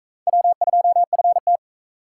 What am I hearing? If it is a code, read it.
Morse code, heard as a keyed tone, about 40 wpm, 700 Hz. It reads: W2FT